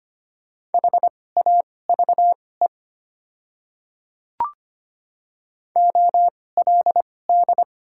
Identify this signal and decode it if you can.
Morse code, heard as a keyed tone, about 25 wpm, spaced as for 18 wpm, 700 Hz. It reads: HAVE OLD
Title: 2x Character Spacing - Top 300 Words - 25wpm